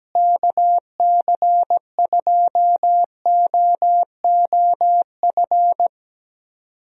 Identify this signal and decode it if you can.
Morse code, heard as a keyed tone, about 17 wpm, 700 Hz. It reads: KC2OOF
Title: USA Call Signs - 6 Characters - Repeated 3x - 17wpm